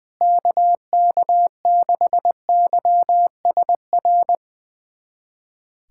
Morse code, heard as a keyed tone, about 20 wpm, 700 Hz. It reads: KK6YSR
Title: USA Callsign DE Callsign - 6 Characters - 20wpm